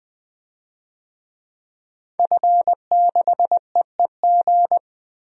Morse code, heard as a keyed tone, about 21 wpm, 700 Hz. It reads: F6EEG
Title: DXCC-100 Call Signs - 5 Characters - 20wpm